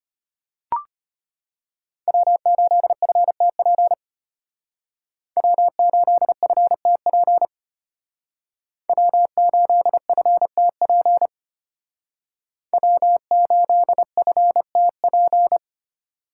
Morse code, heard as a keyed tone, about 30 wpm, 700 Hz. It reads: W8FTP W8FTP W8FTP W8FTP
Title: Speed Racing - USA 1x3 Call Signs 25wpm